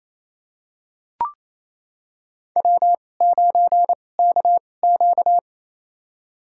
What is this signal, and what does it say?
Morse code, heard as a keyed tone, about 28 wpm, spaced as for 20 wpm, 700 Hz. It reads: W9KQ
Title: USA Call Signs - 4 Characters - 2x Character Spacing - 28wpm